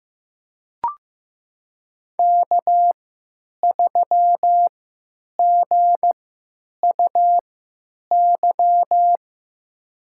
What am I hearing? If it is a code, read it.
Morse code, heard as a keyed tone, about 15 wpm, spaced as for 9 wpm, 700 Hz. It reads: K3GUY
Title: USA Call Signs - 3x Character Spacing - 15wpm